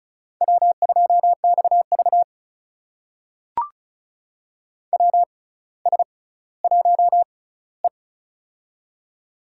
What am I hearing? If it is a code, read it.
Morse code, heard as a keyed tone, about 36 wpm, spaced as for 18 wpm, 700 Hz. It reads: W2XV WS1E
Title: USA Call Signs - 4 Characters - 6x Character Spacing - 35wpm